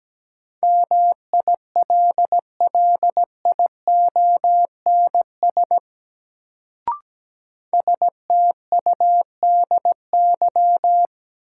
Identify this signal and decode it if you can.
Morse code, heard as a keyed tone, about 17 wpm, 700 Hz. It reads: MILLIONS STUDY
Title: Sets of 5 Words - New Word Drill - 17wpm